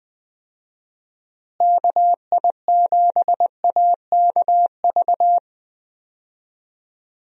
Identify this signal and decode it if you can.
Morse code, heard as a keyed tone, about 20 wpm, 700 Hz. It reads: KI7AKV
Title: USA Call Signs - 6 Characters - Repeated 2x - 20wpm